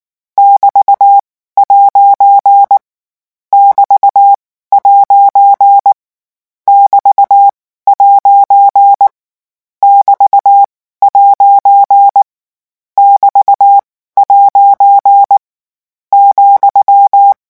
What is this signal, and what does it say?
Morse code, heard as a keyed tone, about 19 wpm, spaced as for 14 wpm, 800 Hz. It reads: <BT>' <BT>' <BT>' <BT> ' <BT> ' ,
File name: morse mp3 / lecon14-vitesse 18